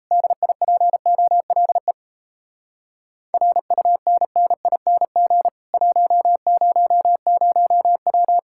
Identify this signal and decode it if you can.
Morse code, heard as a keyed tone, about 38 wpm, 700 Hz. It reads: DIPOLE RUNNING 100W